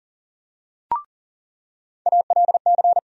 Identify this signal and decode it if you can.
Morse code, heard as a keyed tone, about 37 wpm, 700 Hz. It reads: ALC
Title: USA Call Sign Postfixes - 3 Letters 40wpm